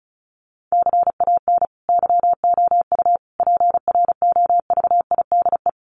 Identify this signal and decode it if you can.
Morse code, heard as a keyed tone, about 34 wpm, 700 Hz. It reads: CAN YOU PROVIDE
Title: Sentences from Top 1500 Words 35wpm